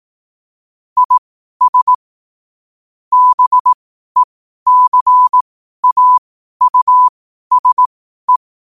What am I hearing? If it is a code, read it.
Morse code, heard as a keyed tone, about 18 wpm, spaced as for 12 wpm, 1000 Hz. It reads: IS BECAUSE
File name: qotdfiles / 12wpm